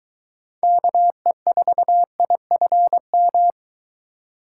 Morse code, heard as a keyed tone, about 23 wpm, 700 Hz. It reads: KE4IFM